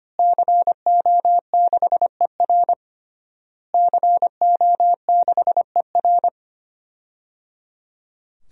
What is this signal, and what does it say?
Morse code, heard as a keyed tone, about 25 wpm, 700 Hz. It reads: CO6ER CO6ER
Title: DXCC-100 Call Signs - 5 Characters - Repeated 3x - 25wpm